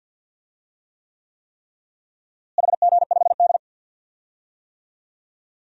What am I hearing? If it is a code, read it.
Morse code, heard as a keyed tone, about 53 wpm, 700 Hz. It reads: HG5D
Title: DXCC-100 Call Signs - 4 Characters - Repeated 3x - 50wpm